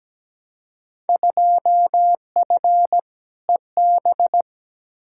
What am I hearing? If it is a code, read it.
Morse code, heard as a keyed tone, about 17 wpm, 700 Hz. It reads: 2F EB
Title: Field Day Exchanges Encore - 17wpm